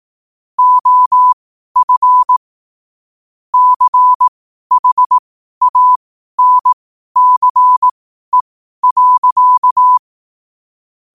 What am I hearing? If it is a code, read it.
Morse code, heard as a keyed tone, about 18 wpm, spaced as for 12 wpm, 1000 Hz. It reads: OF CHANCE.
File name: qotdfiles / 12wpm